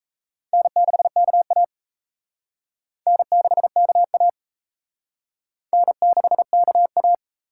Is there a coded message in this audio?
Yes: Morse code from 0.5 to 7.1 s, heard as a keyed tone, about 36 wpm, 700 Hz. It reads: N6KA N6KA N6KA